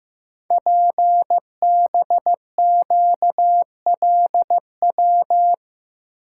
Morse code, heard as a keyed tone, about 15 wpm, 700 Hz. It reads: PBQLW
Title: Code Groups - 5 Letters - 15wpm